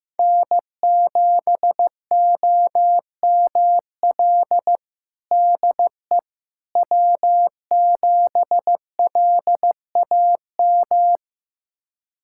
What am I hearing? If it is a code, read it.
Morse code, heard as a keyed tone, about 15 wpm, 700 Hz. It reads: N7OML DE W7LAM
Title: USA Callsign DE Callsign - 5 Characters - 15wpm